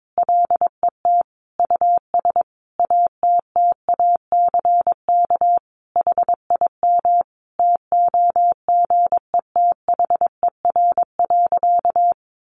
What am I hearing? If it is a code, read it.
Morse code, heard as a keyed tone, about 22 wpm, 700 Hz. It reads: LET US ATTACK HIM TOGETHER.
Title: Sentences from Top 1500 Words 22wpm